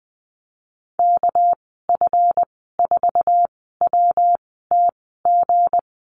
Morse code, heard as a keyed tone, about 19 wpm, spaced as for 14 wpm, 700 Hz. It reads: KF4WTG
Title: 231 Lesson 37g - Letter x - Call Signs 20wpm